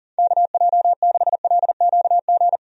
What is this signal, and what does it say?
Morse code, heard as a keyed tone, about 40 wpm, 690 Hz. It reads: KJ6LQG
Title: USA Call Signs - 6 Characters - 40wpm